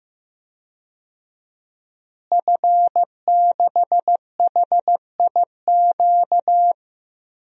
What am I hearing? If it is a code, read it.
Morse code, heard as a keyed tone, about 15 wpm, 700 Hz. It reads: F6HIQ
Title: DXCC-100 Call Signs - 5 Characters - Repeated 2x - 15wpm